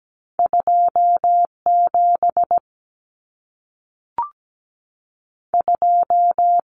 Morse code, heard as a keyed tone, about 17 wpm, 700 Hz. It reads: 27 2
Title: Common Cut Numbers - 2-Digit 17wpm